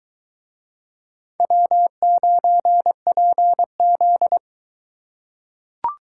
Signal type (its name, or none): Morse code